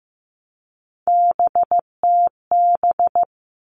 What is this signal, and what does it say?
Morse code, heard as a keyed tone, about 15 wpm, 700 Hz. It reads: BTB
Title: All Cut Numbers - 3-Digit 15wpm